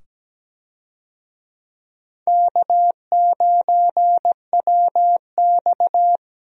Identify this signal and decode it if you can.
Morse code, heard as a keyed tone, about 17 wpm, 700 Hz. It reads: K9WX
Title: USA Call Signs - 4 Characters - 17wpm